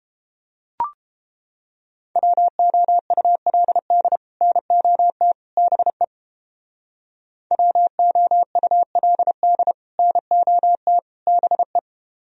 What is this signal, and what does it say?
Morse code, heard as a keyed tone, about 29 wpm, 700 Hz. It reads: WOULD NOT BE WOULD NOT BE